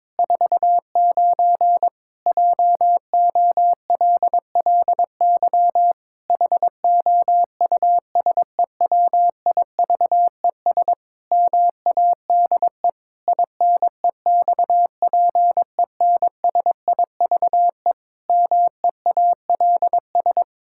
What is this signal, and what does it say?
Morse code, heard as a keyed tone, about 21 wpm, 700 Hz. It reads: 49 JOLLY HOUSEWIVES MADE INEXPENSIVE MEALS
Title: Pangram Challenge v2 - 22wpm